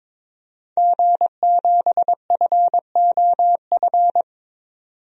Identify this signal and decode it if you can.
Morse code, heard as a keyed tone, about 22 wpm, 700 Hz. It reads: G7FOF
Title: DXCC-100 Call Signs - 5 Characters - 22wpm